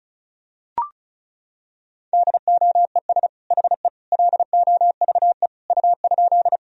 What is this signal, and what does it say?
Morse code, heard as a keyed tone, about 36 wpm, 700 Hz. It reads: DOES HE LOVE U?
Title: Sentences from Top 400 Words and Abbreviations 35wpm